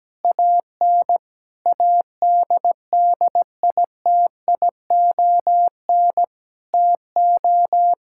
Morse code, heard as a keyed tone, about 17 wpm, 700 Hz. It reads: AN ADDITION TO
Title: Sentences from Top 1000 Words - Review Part 6 - 17wpm